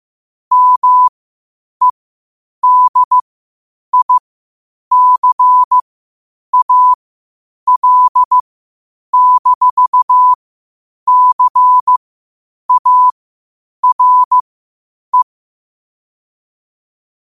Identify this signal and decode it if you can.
Morse code, heard as a keyed tone, about 15 wpm, spaced as for 8 wpm, 1000 Hz. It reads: MEDICAL-CARE
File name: qotdfiles / 8wpm